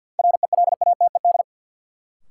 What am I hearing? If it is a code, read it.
Morse code, heard as a keyed tone, about 51 wpm, 700 Hz. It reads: RELATED
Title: Sentences from Top 1000 Words - Review Part 5 - New Word Drill - 50wpm